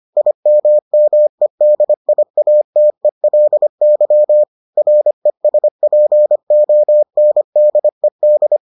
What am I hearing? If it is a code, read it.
Morse code, heard as a keyed tone, about 25 wpm, 600 Hz. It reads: IMMEDIATELY RESPONDED